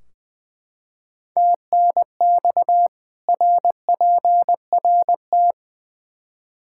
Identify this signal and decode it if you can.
Morse code, heard as a keyed tone, about 20 wpm, 700 Hz. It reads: TNX RPRT
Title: K1USN SST QSO Practice 20wpm